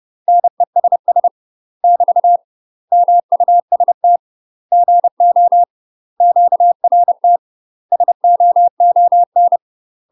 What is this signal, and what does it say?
Morse code, heard as a keyed tone, about 30 wpm, 700 Hz. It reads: NESS <BT> MUST GO QRT SOON
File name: UK MASTER QSOs TRAINING / 30WPM / UK MASTER QSOs 0009